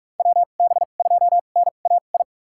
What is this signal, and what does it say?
Morse code, heard as a keyed tone, about 41 wpm, spaced as for 31 wpm, 700 Hz. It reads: WB2NAI
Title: USA Call Signs - 6 Characters - 2x Character Spacing - 45wpm